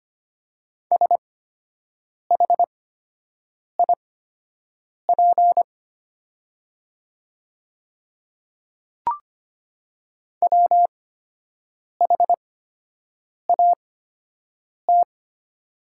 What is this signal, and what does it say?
Morse code, heard as a keyed tone, about 25 wpm, spaced as for 7 wpm, 690 Hz. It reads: SHIP WHAT
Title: 8x Character Spacing - Top 500 Words - 25wpm